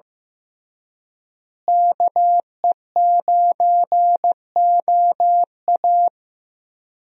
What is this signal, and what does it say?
Morse code, heard as a keyed tone, about 15 wpm, 700 Hz. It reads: KE9OA
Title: USA Call Signs - 5 Characters - Repeated 3x - 15wpm